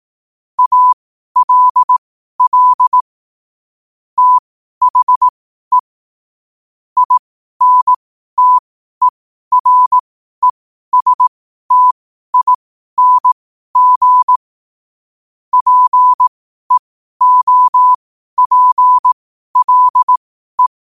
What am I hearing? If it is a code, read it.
Morse code, heard as a keyed tone, about 18 wpm, spaced as for 12 wpm, 1000 Hz. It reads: ALL THE INTERESTING PEOPLE
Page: Quotes for Sun, 27 Apr 2025 in Morse Code at 12 words per minute.